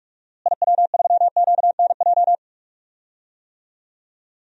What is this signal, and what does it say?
Morse code, heard as a keyed tone, about 46 wpm, 700 Hz. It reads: IW3QNJ
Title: DXCC-100 Call Signs - 6 Characters - Repeated 3x - 45wpm